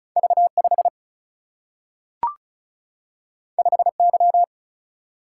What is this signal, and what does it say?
Morse code, heard as a keyed tone, about 36 wpm, 700 Hz. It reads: V5 5Y
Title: Single-Letter Single-Number 35wpm